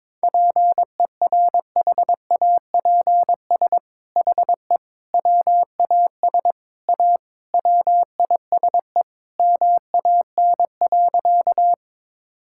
Morse code, heard as a keyed tone, about 22 wpm, 700 Hz. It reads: PERHAPS HE WAS A WISE MAN.
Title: Sentences from Top 1600 Words 22wpm